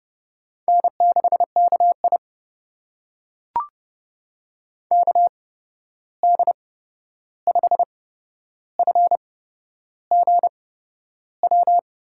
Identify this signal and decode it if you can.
Morse code, heard as a keyed tone, about 30 wpm, spaced as for 11 wpm, 700 Hz. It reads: N6KI KD5FGW
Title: USA Call Signs - 8x Character Spacing - 30wpm